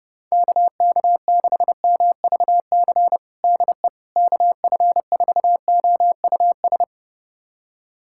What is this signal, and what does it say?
Morse code, heard as a keyed tone, about 30 wpm, 700 Hz. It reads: KK6MVC DE KF4OUS
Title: USA Callsign DE Callsign - 6 Characters - 30wpm